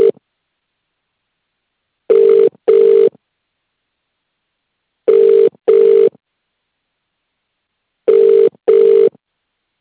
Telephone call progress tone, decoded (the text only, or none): ringback tone